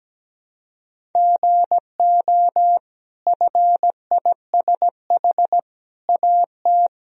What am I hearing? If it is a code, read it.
Morse code, heard as a keyed tone, about 17 wpm, 700 Hz. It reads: GO FISH AT
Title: Sentences from Top 1000 Words - Review Part 1 - 17wpm